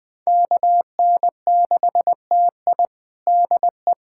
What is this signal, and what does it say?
Morse code, heard as a keyed tone, about 21 wpm, 700 Hz. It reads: KN6TI DE
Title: USA Callsign DE Callsign - 5 Characters - 20wpm